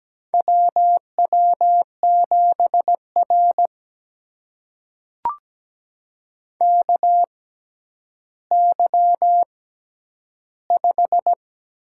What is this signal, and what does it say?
Morse code, heard as a keyed tone, about 17 wpm, spaced as for 10 wpm, 700 Hz. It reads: WW7R KY5
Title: USA Call Signs - 4 Characters - 6x Character Spacing - 17wpm